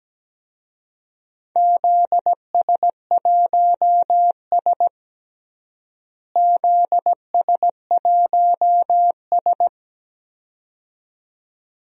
Morse code, heard as a keyed tone, about 17 wpm, 700 Hz. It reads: ZS1S ZS1S
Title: DXCC-100 Call Signs - 4 Characters - Repeated 2x - 17wpm